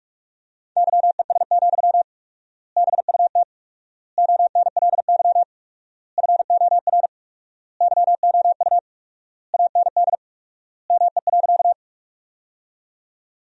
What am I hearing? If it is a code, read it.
Morse code, heard as a keyed tone, about 46 wpm, 700 Hz. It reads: YES, BUT ONLY FOR YOU AND ME.
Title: 4x Word-Spacing - Sentences from Top 500 Words - 45wpm